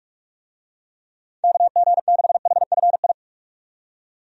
Morse code, heard as a keyed tone, about 47 wpm, 700 Hz. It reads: KG6HFI
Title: USA Call Signs - 6 Characters - 45wpm